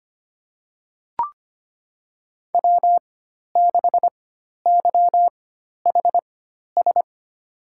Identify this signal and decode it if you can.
Morse code, heard as a keyed tone, about 25 wpm, spaced as for 12 wpm, 700 Hz. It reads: W6YHS
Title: USA Call Signs - 5 Characters - 4x Character Spacing - 25wpm